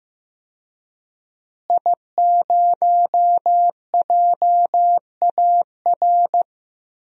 Morse code, heard as a keyed tone, about 15 wpm, 700 Hz. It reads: I0JAR
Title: DXCC-100 Call Signs - 5 Characters - Repeated 3x - 15wpm